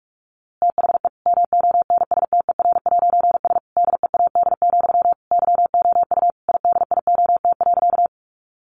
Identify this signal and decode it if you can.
Morse code, heard as a keyed tone, about 46 wpm, 700 Hz. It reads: THE MONSTER'S DEAD, YOU IDIOT.
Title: A Sound Of Thunder 45wpm